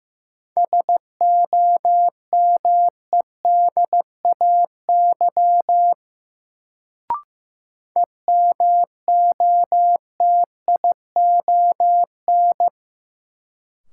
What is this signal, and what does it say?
Morse code, heard as a keyed tone, about 15 wpm, 700 Hz. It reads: SOMEDAY EMOTION
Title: Sentences from Top 1600 Words - New Word Drill Alt - 15wpm